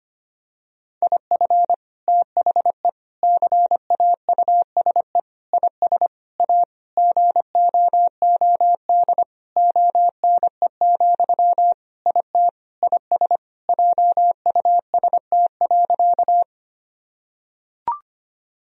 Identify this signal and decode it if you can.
Morse code, heard as a keyed tone, about 25 wpm, 700 Hz. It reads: IF THE CAUSE IS A GOOD ONE, IT IS JUST.
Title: Sentences from Top 1000 Words - Review Part 1 - 25wpm